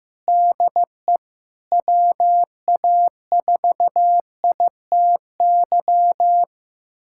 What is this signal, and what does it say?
Morse code, heard as a keyed tone, about 16 wpm, 700 Hz. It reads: DE WA4ITY
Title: USA Callsign DE Callsign - 6 Characters - 15wpm